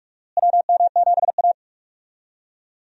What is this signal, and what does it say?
Morse code, heard as a keyed tone, about 41 wpm, 700 Hz. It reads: WM7A
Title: USA Call Signs - 4 Characters - Repeated 3x - 45wpm